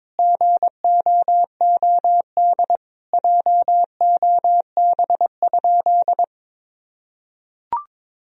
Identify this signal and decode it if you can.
Morse code, heard as a keyed tone, about 22 wpm, 700 Hz. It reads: GOOD JOB?
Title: Sentences from Top 1000 Words - Review Part 3 - 22wpm